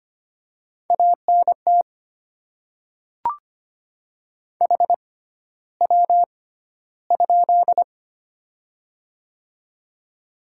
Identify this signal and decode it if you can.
Morse code, heard as a keyed tone, about 25 wpm, spaced as for 13 wpm, 700 Hz. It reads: ANT HW?
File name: Top 20 QSO Elements - 6x Character Spacing - 25wpm